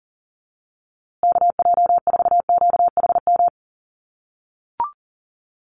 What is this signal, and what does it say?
Morse code, heard as a keyed tone, about 40 wpm, 700 Hz. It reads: KJ4QHM
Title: USA Call Signs Repeated 2x 40wpm